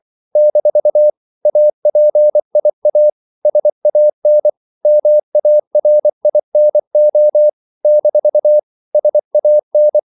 Morse code, heard as a keyed tone, about 24 wpm, 600 Hz. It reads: - APIA SAN MARINO - SAN
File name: CW / Practice / NationalCapitals24